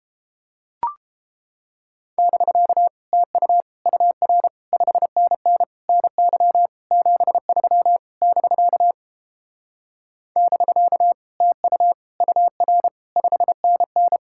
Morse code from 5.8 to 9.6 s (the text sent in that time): NY 73 <BK>